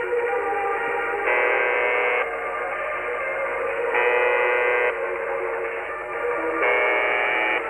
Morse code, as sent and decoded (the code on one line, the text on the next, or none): none